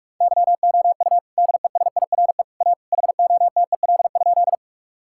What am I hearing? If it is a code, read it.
Morse code, heard as a keyed tone, about 41 wpm, 700 Hz. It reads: YOU DESIRE A HOTEL?
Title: Sentences from Top 1600 Words 45wpm